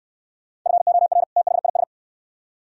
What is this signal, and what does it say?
Morse code, heard as a keyed tone, about 64 wpm, 700 Hz. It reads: FOR THIS